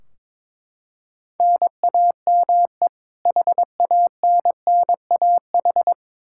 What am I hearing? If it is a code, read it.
Morse code, heard as a keyed tone, about 22 wpm, 700 Hz. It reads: NAME HANNAH
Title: Practice QSO - Exchange 1 - Beginning - 22wpm